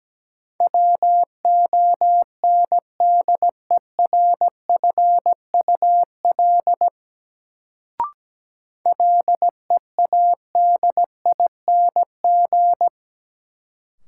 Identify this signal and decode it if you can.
Morse code, heard as a keyed tone, about 17 wpm, 700 Hz. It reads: WONDERFUL LEADING
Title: Sentences from Top 1100 Words - New Word Drill - 17wpm